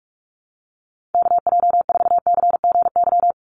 Morse code, heard as a keyed tone, about 44 wpm, 700 Hz. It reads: KJ4CGY
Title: USA Call Signs 45wpm